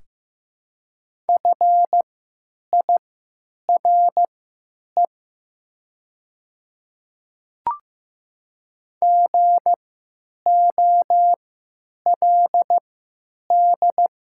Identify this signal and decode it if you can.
Morse code, heard as a keyed tone, about 16 wpm, spaced as for 9 wpm, 700 Hz. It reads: FIRE GOLD
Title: 3x Character Spacing - Top 500 Words - 15wpm